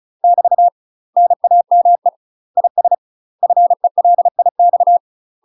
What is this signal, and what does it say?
Morse code, heard as a keyed tone, about 36 wpm, 700 Hz. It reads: <BT> NAME IS FELIX
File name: UK MASTER QSOs TRAINING / 35WPM / UK MASTER QSOs 0036